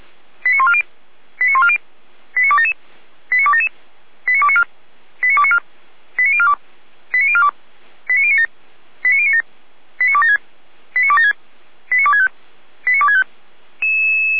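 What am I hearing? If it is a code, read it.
Five-tone selective call (ZVEI), heard as quick runs of five tones, tones of about 70 ms. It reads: 89140 89140 89277 89277 89294 89294 89042 89042 89097 89097 89276 89276 89265 89265
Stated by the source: ZVEI five-tone selective call received on SDR.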